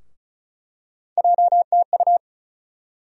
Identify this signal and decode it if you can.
Morse code, heard as a keyed tone, about 36 wpm, 700 Hz. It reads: JTU
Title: USA Call Sign Postfixes - 3 Letters 35wpm